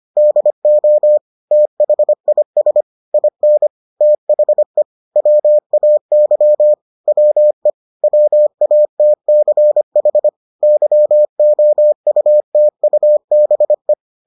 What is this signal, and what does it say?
Morse code, heard as a keyed tone, about 25 wpm, 600 Hz. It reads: DO THIS IN THE WAY WE WATCH YOUTUBE